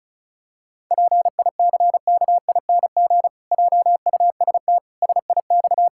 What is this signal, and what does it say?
Morse code, heard as a keyed tone, about 36 wpm, 700 Hz. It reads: PICKING JUST SIX
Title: Pangram Challenge v2 - 35wpm